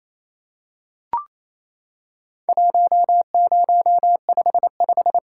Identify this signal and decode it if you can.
Morse code, heard as a keyed tone, about 28 wpm, 700 Hz. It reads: 1055
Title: Worked All Europe DX Contest - QTC Exchange - 28wpm